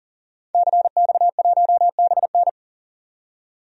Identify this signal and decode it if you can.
Morse code, heard as a keyed tone, about 40 wpm, 700 Hz. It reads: CX1BN
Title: DXCC-100 Call Signs - 5 Characters - 40wpm